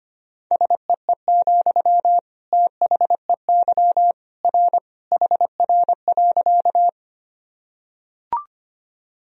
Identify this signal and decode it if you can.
Morse code, heard as a keyed tone, about 25 wpm, 700 Hz. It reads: SEE, THEY R HR.
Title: Sentences from Top 200 Words and Abbreviations 25wpm